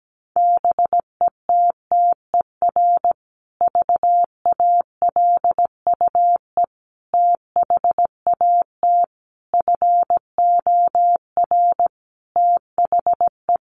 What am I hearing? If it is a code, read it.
Morse code, heard as a keyed tone, about 17 wpm, 700 Hz. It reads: BETTER VALUE THAT FOR THE
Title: Sentences from Top 600 Words 17wpm